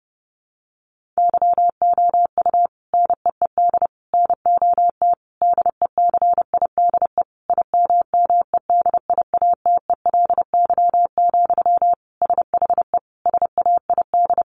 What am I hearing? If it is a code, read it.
Morse code, heard as a keyed tone, about 30 wpm, 700 Hz. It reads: YOU NEED NOT DECIDE IMMEDIATELY, SHE SAID
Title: Sentences from Top 1500 Words 30wpm